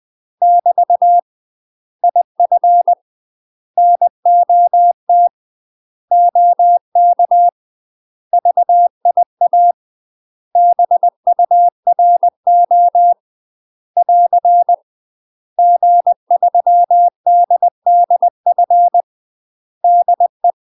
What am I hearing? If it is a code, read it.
Morse code, heard as a keyed tone, about 20 wpm, 700 Hz. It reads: <BT> IF NOT OK VIA BURO <AR> G3DDF DE
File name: UK MASTER QSOs TRAINING / 20WPM / UK MASTER QSOs 0050